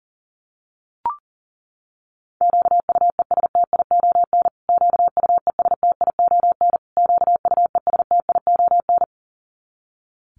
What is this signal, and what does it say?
Morse code, heard as a keyed tone, about 40 wpm, 700 Hz. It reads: QUESTION QUESTION QUESTION
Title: Top 500 Words Repeated 40wpm